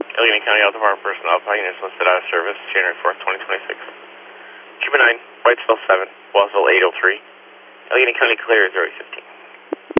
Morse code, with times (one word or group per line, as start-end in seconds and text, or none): none